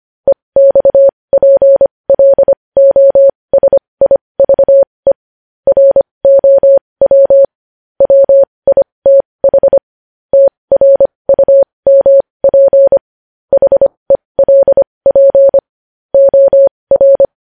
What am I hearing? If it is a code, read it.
Morse code, heard as a keyed tone, about 25 wpm, spaced as for 20 wpm, 550 Hz. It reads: EXPLOSIVE ROW WITH TRUMP HELP OR
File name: podcast / News headlines-1x-25x20